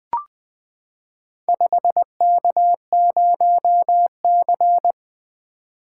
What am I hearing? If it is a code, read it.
Morse code, heard as a keyed tone, about 20 wpm, 700 Hz. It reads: 5K0C
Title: Four Letter-Number Combinations - 20wpm